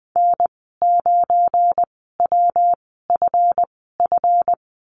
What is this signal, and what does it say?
Morse code, heard as a keyed tone, about 20 wpm, spaced as for 15 wpm, 700 Hz. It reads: N9WFF
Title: 186 Lesson 30g - Letter b - Call Signs 20wpm